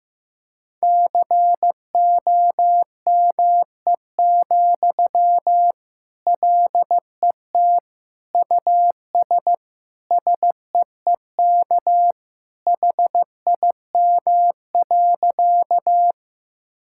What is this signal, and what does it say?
Morse code, heard as a keyed tone, about 15 wpm, 700 Hz. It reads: COME, LET US SEEK HIM.
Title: Sentences from Top 1600 Words 15wpm